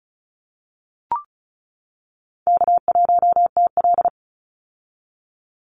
Morse code, heard as a keyed tone, about 36 wpm, 700 Hz. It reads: K1TL
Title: Call Signs - 35wpm